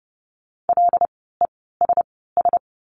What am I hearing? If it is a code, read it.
Morse code, heard as a keyed tone, about 30 wpm, spaced as for 17 wpm, 700 Hz. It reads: LESS
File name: Lesson 16f - Letter c - Words 30wpm